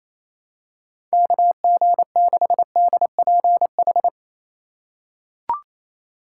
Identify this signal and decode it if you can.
Morse code, heard as a keyed tone, about 28 wpm, 700 Hz. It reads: KG6DPH
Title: USA Call Signs - 3x Character Spacing - 28wpm